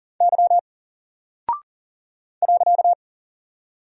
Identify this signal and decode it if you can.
Morse code, heard as a keyed tone, about 40 wpm, 700 Hz. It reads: Y .
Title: Single Character (Letter:Number:Punctuation:Prosign) Alt 40wpm